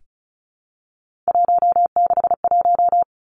Morse code, heard as a keyed tone, about 36 wpm, 700 Hz. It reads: 161
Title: Three Challenging Letter-Number Combinations 35wpm